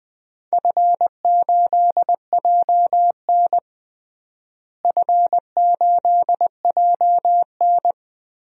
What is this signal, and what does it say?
Morse code, heard as a keyed tone, about 20 wpm, 700 Hz. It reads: F8JN F8JN
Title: DXCC-100 Call Signs - 4 Characters - Repeated 3x - 20wpm